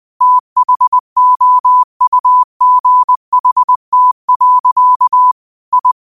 Morse code, heard as a keyed tone, about 20 wpm, 1000 Hz. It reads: THOUGHT. I